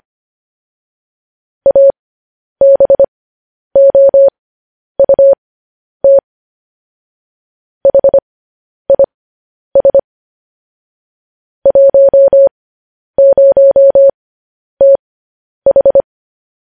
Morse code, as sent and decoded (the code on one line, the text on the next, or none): .- -... --- ..- - / .... .. ... / .---- ----- - ....
ABOUT HIS 10TH